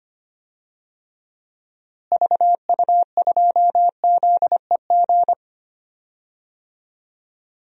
Morse code, heard as a keyed tone, about 25 wpm, 700 Hz. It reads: VU2ZEG
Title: DXCC-100 Call Signs - 6 Characters - Repeated 3x - 25wpm